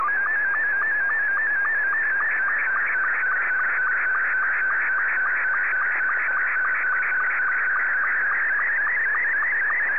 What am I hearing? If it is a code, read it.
SSTV (slow-scan television), Scottie 2.